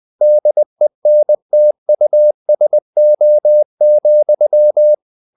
Morse code, heard as a keyed tone, about 20 wpm, 600 Hz. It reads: DENTUSO,
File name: the old man and the sea part 2 20wpm